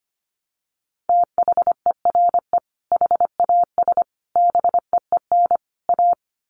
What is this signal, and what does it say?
Morse code, heard as a keyed tone, about 25 wpm, 700 Hz. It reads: THERE HAS BEEN A
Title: Sets of 4 Words 25wpm